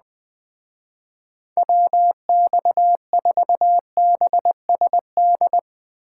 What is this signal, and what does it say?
Morse code, heard as a keyed tone, about 19 wpm, 700 Hz. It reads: WX4BSD
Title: Speed Racing - USA 2x3 Call Signs 17wpm